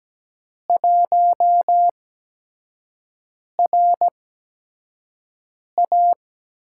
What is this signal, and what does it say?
Morse code, heard as a keyed tone, about 17 wpm, spaced as for 5 wpm, 700 Hz. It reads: 1RA
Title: USA Call Signs - 5 Characters - 8x Character Spacing - 17wpm